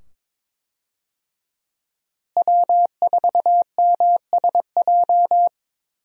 Morse code, heard as a keyed tone, about 22 wpm, 700 Hz. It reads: W4MSJ
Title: USA Call Signs - 5 Characters - Repeated 2x - 22wpm